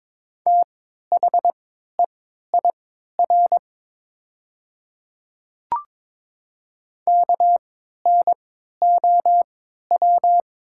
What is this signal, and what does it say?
Morse code, heard as a keyed tone, about 22 wpm, spaced as for 12 wpm, 700 Hz. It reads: THEIR KNOW
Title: 3x Character Spacing - Top 100 Words - 22wpm